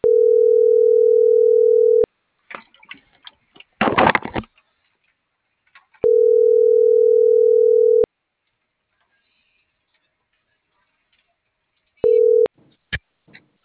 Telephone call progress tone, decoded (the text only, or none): ringback tone